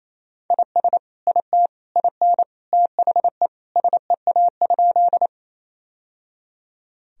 Morse code, heard as a keyed tone, about 28 wpm, 700 Hz. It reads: IS IT IN THE SEA?
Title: Sentences from Top 1000 Words - Review Part 1 - 28wpm